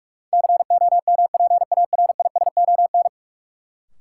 Morse code, heard as a keyed tone, about 47 wpm, 700 Hz. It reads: COMPARISON
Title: Sentences from Top 1600 Words - New Word Drill - 45wpm